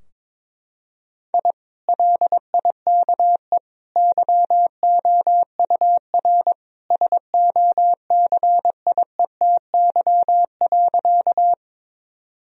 Morse code, heard as a keyed tone, about 22 wpm, 700 Hz. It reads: I LIKE YOUR SOCIETY.
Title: Sentences from Top 1000 Words - Review Part 4 - 22wpm